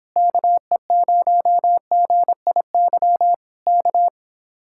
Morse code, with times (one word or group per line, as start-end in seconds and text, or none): 0.2-3.3 KE0GIY
3.7-4.1 K